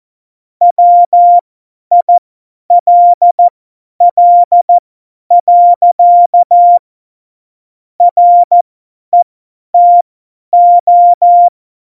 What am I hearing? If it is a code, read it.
Morse code, heard as a keyed tone, about 14 wpm, spaced as for 10 wpm, 700 Hz. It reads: WILL. RETO
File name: Morsekurs / Witze / Witz 10. Oktober 17, Tempo 50